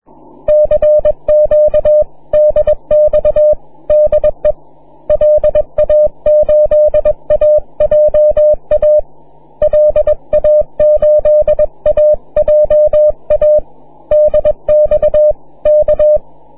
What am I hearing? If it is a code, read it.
Morse code, heard as a keyed tone, about 21 wpm, 600 Hz. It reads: CQDX DE LA8AJA LA8AJA DX K
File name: wp / ji2evl 6m cw